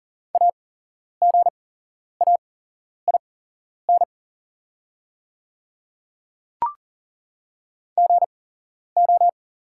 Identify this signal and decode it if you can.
Morse code, heard as a keyed tone, about 40 wpm, spaced as for 11 wpm, 700 Hz. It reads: AGAIN GO
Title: 8x Character Spacing - Top 300 Words - 40wpm